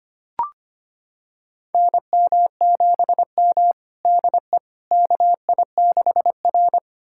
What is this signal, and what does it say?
Morse code, heard as a keyed tone, about 25 wpm, 700 Hz. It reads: NM7M DE KI6R
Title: USA Callsign DE Callsign - 4 Characters - 25wpm